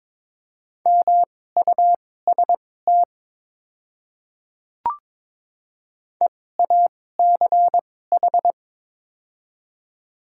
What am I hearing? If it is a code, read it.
Morse code, heard as a keyed tone, about 21 wpm, spaced as for 16 wpm, 700 Hz. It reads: MUST EACH
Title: 2x Character Spacing - Top 200 Words - 22wpm